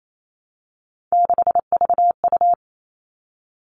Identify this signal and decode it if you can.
Morse code, heard as a keyed tone, about 28 wpm, 700 Hz. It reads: 6VU
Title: All Cut Numbers - 3-Digit 28wpm